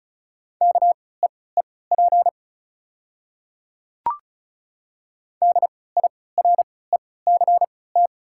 Morse code, heard as a keyed tone, about 36 wpm, spaced as for 20 wpm, 700 Hz. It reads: KEEP DIRECT
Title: 3x Character Spacing - Top 500 Words - 35wpm